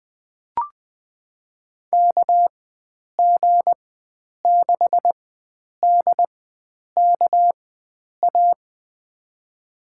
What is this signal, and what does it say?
Morse code, heard as a keyed tone, about 20 wpm, spaced as for 9 wpm, 700 Hz. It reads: KG6DKA
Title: USA Call Signs - 6 Characters - 4x Character Spacing - 20wpm